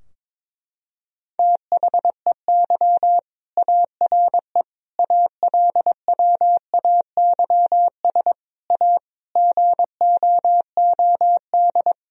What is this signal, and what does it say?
Morse code, heard as a keyed tone, about 22 wpm, 700 Hz. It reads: THEY ARE ALWAYS A GOOD
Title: Sentences from Top 1000 Words - Review Part 4 - 22wpm